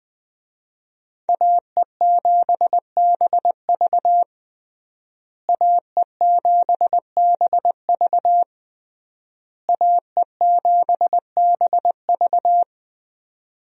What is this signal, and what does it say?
Morse code, heard as a keyed tone, about 20 wpm, 700 Hz. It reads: AE7BV AE7BV AE7BV
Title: USA Call Signs - 5 Characters - Repeated 3x - 20wpm